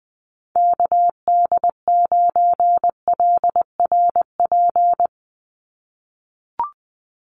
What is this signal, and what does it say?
Morse code, heard as a keyed tone, about 20 wpm, 700 Hz. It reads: KD9LRP
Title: USA Call Signs 20wpm